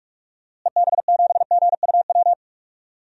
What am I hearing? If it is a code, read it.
Morse code, heard as a keyed tone, about 46 wpm, 690 Hz. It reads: EB7GUW
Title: DXCC-100 Call Signs - 6 Characters - Repeated 2x - 45wpm